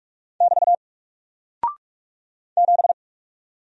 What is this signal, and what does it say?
Morse code, heard as a keyed tone, about 44 wpm, 700 Hz. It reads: <BT> 7
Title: Single Character (Letter:Number:Punctuation:Prosign) Alt 45wpm